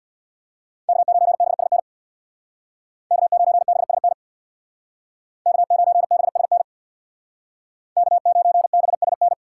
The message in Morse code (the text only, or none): K9BSN K9BSN K9BSN K9BSN